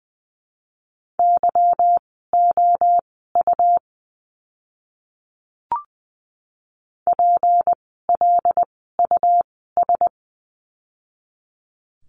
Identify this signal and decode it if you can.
Morse code, heard as a keyed tone, about 20 wpm, spaced as for 14 wpm, 700 Hz. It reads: YOU PLUS
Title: 153 Lesson 25f - Letter p - Words 20wpm